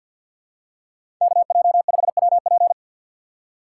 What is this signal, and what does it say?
Morse code, heard as a keyed tone, about 47 wpm, 700 Hz. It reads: KJ5WP
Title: USA Call Signs - 5 Characters - 2x Character Spacing - 50wpm